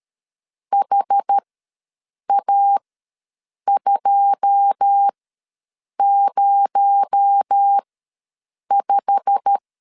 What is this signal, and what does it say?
Morse code, heard as a keyed tone, about 13 wpm, spaced as for 7 wpm, 800 Hz. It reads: HA205
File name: LEZIONI / 9 2 ws 12wpm 3406